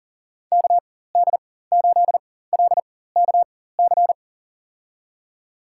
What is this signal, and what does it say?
Morse code, heard as a keyed tone, about 40 wpm, spaced as for 19 wpm, 700 Hz. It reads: KD8LKC
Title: USA Call Signs - 4x Character Spacing - 40wpm